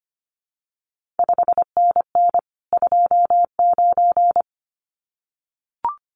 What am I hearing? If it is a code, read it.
Morse code, heard as a keyed tone, about 25 wpm, 700 Hz. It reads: 5NN 29
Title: CQ WW DX Contest Exchange Encore - 25wpm